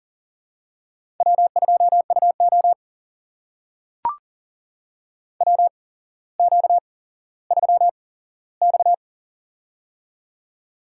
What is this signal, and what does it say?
Morse code, heard as a keyed tone, about 40 wpm, spaced as for 17 wpm, 700 Hz. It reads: W2UO WQ3X
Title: USA Call Signs - 4 Characters - 8x Character Spacing - 40wpm